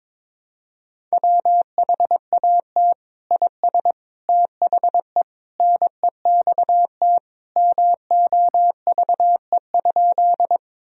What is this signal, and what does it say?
Morse code, heard as a keyed tone, about 22 wpm, 700 Hz. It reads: WHAT IS THE NEXT MOVE?
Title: Sentences from Top 1000 Words - Review Part 1 - 22wpm